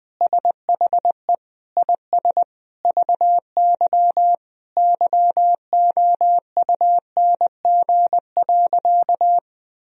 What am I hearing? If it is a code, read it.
Morse code, heard as a keyed tone, about 21 wpm, 700 Hz. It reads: SHE IS VY YOUNG.
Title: Sentences from Top 400 Words and Abbreviations 20wpm